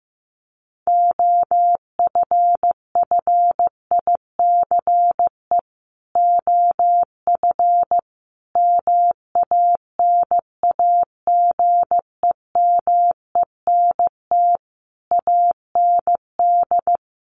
Morse code, heard as a keyed tone, about 15 wpm, 700 Hz. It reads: OFFICE OF MANAGEMENT AND
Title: Sets of 5 Words Encore2 15wpm